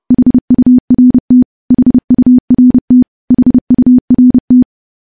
Morse code, heard as a keyed tone, about 32 wpm, 250 Hz. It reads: HURT HURT HURT
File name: morse / morse-training-30WPM